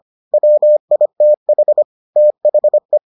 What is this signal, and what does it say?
Morse code, heard as a keyed tone, about 25 wpm, 600 Hz. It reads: WITH THE